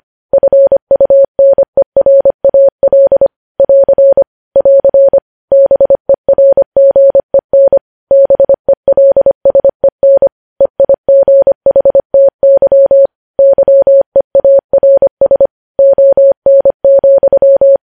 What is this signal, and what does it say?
Morse code, heard as a keyed tone, about 25 wpm, 550 Hz. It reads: FUNERAL <AR> <AR> BERGEN BELSEN EIGHTY YEARS ON,
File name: podcast / News headlines-1x-25x25